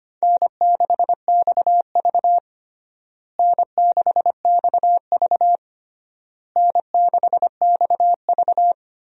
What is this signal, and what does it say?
Morse code, heard as a keyed tone, about 25 wpm, 700 Hz. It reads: N6XV N6XV N6XV
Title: USA Call Signs - 4 Characters - Repeated 3x - 25wpm